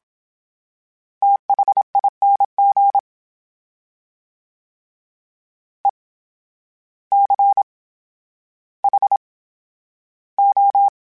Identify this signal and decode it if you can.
Morse code, heard as a keyed tone, about 27 wpm, spaced as for 12 wpm, 800 Hz. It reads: THING ECHO
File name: Special INGs / Special INGs-mp3s / VerySlowToFaster Ing @25